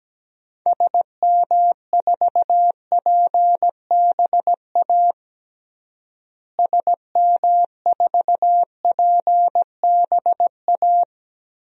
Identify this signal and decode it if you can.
Morse code, heard as a keyed tone, about 17 wpm, 700 Hz. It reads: SM4PBA SM4PBA
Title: DXCC-100 Call Signs - 6 Characters - Repeated 3x - 17wpm